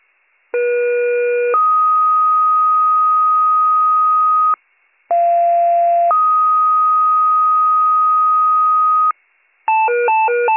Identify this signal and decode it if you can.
Two-tone page, heard as a one-second tone, then a three-second tone, about 480 then 1250 Hz, about 690 then 1250 Hz. It is heard from 0.5 to 9.1 s.